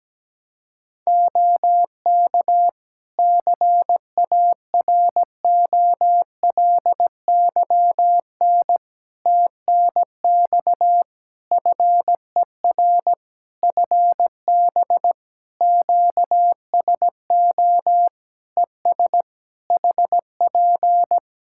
Morse code, heard as a keyed tone, about 17 wpm, 700 Hz. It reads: OK CAROLYN TNX FER FB QSO ES HP
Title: Practice QSO - Exchange 4 - Ending - 17wpm